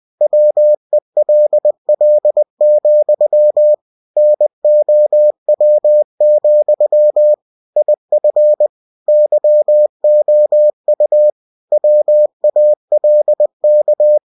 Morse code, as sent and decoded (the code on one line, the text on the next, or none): .-- . .-.. .-.. --..-- / -. --- .-- --..-- / .. ..-. / -.-- --- ..- / .-- .- .-.. -.-
WELL, NOW, IF YOU WALK